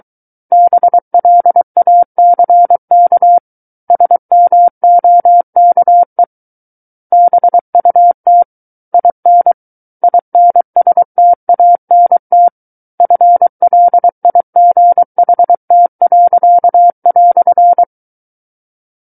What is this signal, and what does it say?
Morse code, heard as a keyed tone, about 23 wpm, 700 Hz. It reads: BLACK SMOKE BUT IN INSTANT FLIGHT."
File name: Book One / 14-Chapter 14 - 23 WPM